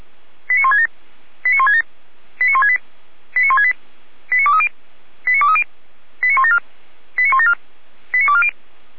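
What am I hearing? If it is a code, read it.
Five-tone selective call (ZVEI), heard as quick runs of five tones, tones of about 70 ms. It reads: 89167 89167 89168 89168 89230 89230 89174 89174 89240
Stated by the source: ZVEI five-tone call received on an SDR, from feuerwehr-orlamuende.org (Orlamünde fire brigade).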